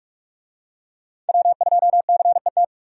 Morse code, heard as a keyed tone, about 43 wpm, 700 Hz. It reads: W2CET
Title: USA Call Signs - 5 Characters - 45wpm